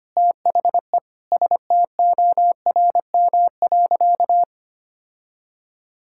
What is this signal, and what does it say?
Morse code, heard as a keyed tone, about 25 wpm, 700 Hz. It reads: THE STORM.
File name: practice / 25wpm / Sentences with 4-Word Phrases - 25wpm